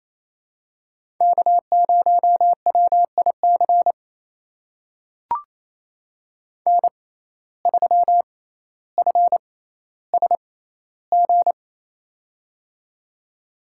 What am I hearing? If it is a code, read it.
Morse code, heard as a keyed tone, about 28 wpm, spaced as for 14 wpm, 700 Hz. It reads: K0WIC N3FSG
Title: USA Call Signs - 5 Characters - 6x Character Spacing - 28wpm